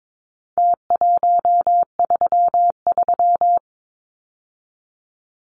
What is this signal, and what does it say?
Morse code, heard as a keyed tone, about 22 wpm, 700 Hz. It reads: T133
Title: Numbers - Contest Style - 4-Digit 22wpm